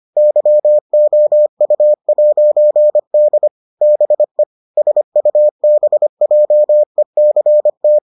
Morse code, heard as a keyed tone, about 25 wpm, 600 Hz. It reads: YOU'D BE SUBJECT